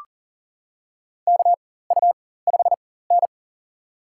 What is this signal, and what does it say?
Morse code, heard as a keyed tone, about 40 wpm, spaced as for 19 wpm, 700 Hz. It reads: KU5N
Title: USA Call Signs - 4x Character Spacing - 40wpm